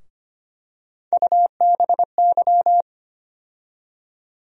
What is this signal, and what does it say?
Morse code, heard as a keyed tone, about 25 wpm, 700 Hz. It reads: UBY
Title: USA Call Sign Postfixes - 3 Letters 25wpm